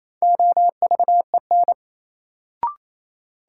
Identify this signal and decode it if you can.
Morse code, heard as a keyed tone, about 28 wpm, 700 Hz. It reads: OVEN
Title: Sentences from Top 1200 Words - New Word Drill Alt - 28wpm